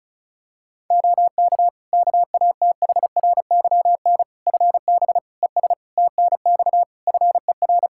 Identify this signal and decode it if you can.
Morse code, heard as a keyed tone, about 36 wpm, 700 Hz. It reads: OK KATHRYN FB ES TNX FER
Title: K1USN SST QSO Practice 35wpm